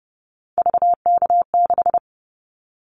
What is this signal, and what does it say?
Morse code, heard as a keyed tone, about 30 wpm, 700 Hz. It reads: VK6
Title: Callsign Prefixes for Top 50 DXCC - 30wpm